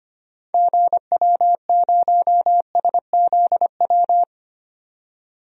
Morse code, heard as a keyed tone, about 25 wpm, 700 Hz. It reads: GW0SZW
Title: DXCC-100 Call Signs - 6 Characters - 25wpm